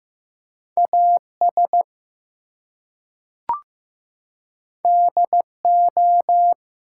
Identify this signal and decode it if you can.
Morse code, heard as a keyed tone, about 15 wpm, 700 Hz. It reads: AS DO
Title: Two Letter Words - 15wpm